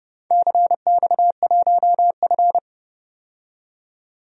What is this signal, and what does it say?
Morse code, heard as a keyed tone, about 28 wpm, 700 Hz. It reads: CX1F
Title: DXCC-100 Call Signs - 4 Characters - 30wpm